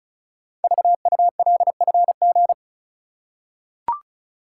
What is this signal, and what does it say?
Morse code, heard as a keyed tone, about 36 wpm, 700 Hz. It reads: VULFG
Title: Code Groups - 5 Letters - 35wpm